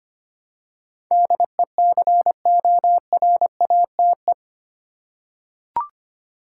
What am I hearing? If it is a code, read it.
Morse code, heard as a keyed tone, about 25 wpm, 700 Hz. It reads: DECORATE
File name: Eight Letter Words - 25wpm